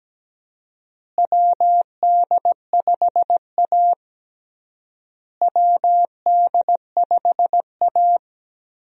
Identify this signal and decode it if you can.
Morse code, heard as a keyed tone, about 17 wpm, 700 Hz. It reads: WD5A WD5A
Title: USA Call Signs - 4 Characters - Repeated 3x - 17wpm